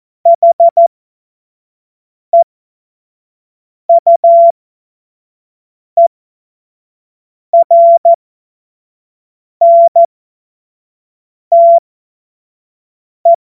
Morse code, heard as a keyed tone, about 14 wpm, spaced as for 5 wpm, 680 Hz. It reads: HEUERNTE
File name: Witz-16-September-2025-25-F70